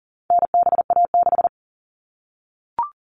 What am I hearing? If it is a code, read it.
Morse code, heard as a keyed tone, about 40 wpm, 700 Hz. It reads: NBA6
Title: All Cut Numbers - 4-Digit 40wpm